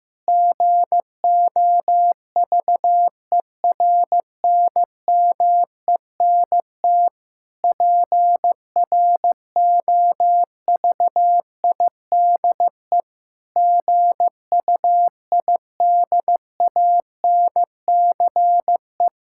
Morse code, heard as a keyed tone, about 15 wpm, 700 Hz. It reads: GOVERNMENT PROVIDE GUIDANCE
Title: Sentences from Top 1600 Words 15wpm